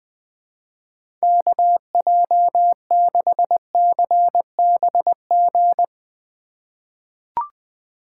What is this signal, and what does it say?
Morse code, heard as a keyed tone, about 20 wpm, 700 Hz. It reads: KJ6CBG
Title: USA Call Signs - 6 Characters - 4x Character Spacing - 20wpm